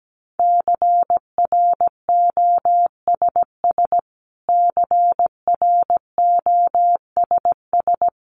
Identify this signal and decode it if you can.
Morse code, heard as a keyed tone, about 17 wpm, 700 Hz. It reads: CROSS CROSS
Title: Top 500 Words Repeated 17wpm